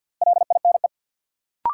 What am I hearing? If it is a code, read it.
Morse code, heard as a keyed tone, about 52 wpm, 700 Hz. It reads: LINE
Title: Sentences from Top 1000 Words - Review Part 1 - New Word Drill - 50wpm